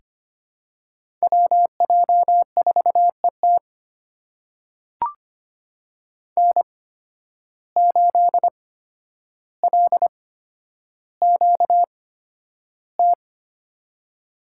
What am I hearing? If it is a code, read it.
Morse code, heard as a keyed tone, about 25 wpm, spaced as for 11 wpm, 700 Hz. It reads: WJ4ET N8LQT
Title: USA Call Signs - 5 Characters - 8x Character Spacing - 25wpm